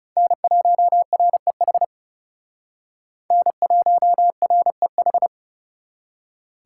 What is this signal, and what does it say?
Morse code, heard as a keyed tone, about 37 wpm, 700 Hz. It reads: N1REH N1REH
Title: Speed Racing - USA 1x3 Call Signs 30wpm